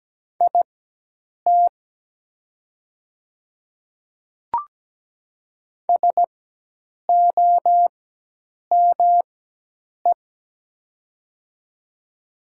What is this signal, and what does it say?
Morse code, heard as a keyed tone, about 17 wpm, spaced as for 8 wpm, 700 Hz. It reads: IT SOME
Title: 4x Character Spacing - Top 500 Words - 17wpm